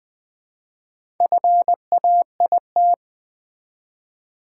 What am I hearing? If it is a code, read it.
Morse code, heard as a keyed tone, about 20 wpm, 700 Hz. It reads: FAIT
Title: Build A Word - 1 Letter At A Time - 5-Letter Words - 20wpm